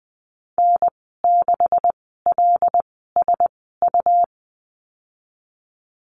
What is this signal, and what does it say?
Morse code, heard as a keyed tone, about 20 wpm, spaced as for 15 wpm, 700 Hz. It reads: N6LSU
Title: 173 Lesson 28g - Number 9 - Call Signs 20wpm